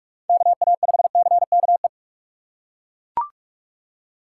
Morse code, heard as a keyed tone, about 41 wpm, 700 Hz. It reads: KA5CKE